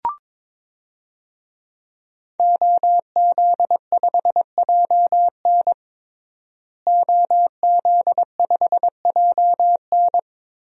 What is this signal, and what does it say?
Morse code, heard as a keyed tone, about 22 wpm, 700 Hz. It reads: OZ5JN OZ5JN
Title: DXCC-100 Call Signs - 5 Characters - Repeated 3x - 22wpm